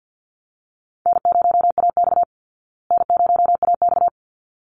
Morse code, heard as a keyed tone, about 48 wpm, 700 Hz. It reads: N0AX N0AX
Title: Easy Call Signs Repeated 2x - 50wpm